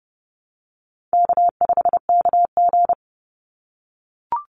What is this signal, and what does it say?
Morse code, heard as a keyed tone, about 30 wpm, 700 Hz. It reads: K5KG
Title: Call Signs - 30wpm